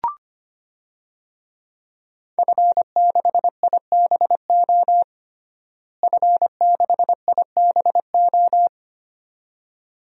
Morse code, heard as a keyed tone, about 25 wpm, 700 Hz. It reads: F6IBO F6IBO
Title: DXCC-100 Call Signs - 5 Characters - Repeated 2x - 25wpm